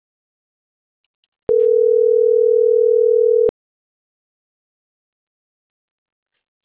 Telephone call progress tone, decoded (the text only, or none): ringback tone